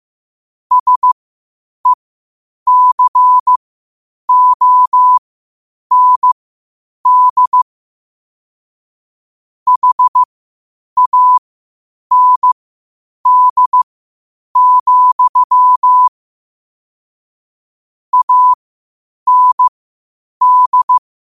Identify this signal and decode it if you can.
Morse code, heard as a keyed tone, about 15 wpm, spaced as for 8 wpm, 1000 Hz. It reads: SECOND HAND, AND
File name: qotdfiles / 8wpm